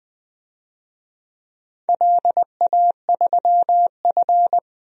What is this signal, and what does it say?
Morse code, heard as a keyed tone, about 20 wpm, 700 Hz. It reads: LA3F
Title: DXCC-100 Call Signs - 4 Characters - 20wpm